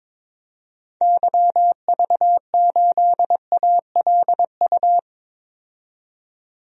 Morse code, heard as a keyed tone, about 22 wpm, 700 Hz. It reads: YV8ALU
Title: DXCC-100 Call Signs - 6 Characters - 22wpm